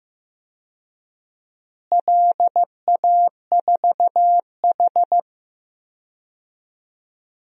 Morse code, heard as a keyed tone, about 15 wpm, 700 Hz. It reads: LA4H
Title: DXCC-100 Call Signs - 4 Characters - Repeated 2x - 15wpm